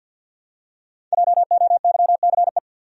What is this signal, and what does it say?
Morse code, heard as a keyed tone, about 48 wpm, 700 Hz. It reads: JOYCE
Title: USA Common Names - 50wpm